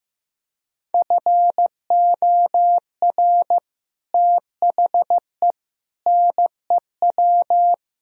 Morse code, heard as a keyed tone, about 15 wpm, 700 Hz. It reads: FOR THE NEW
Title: Speed-Racing - Sets of 5 Words 15wpm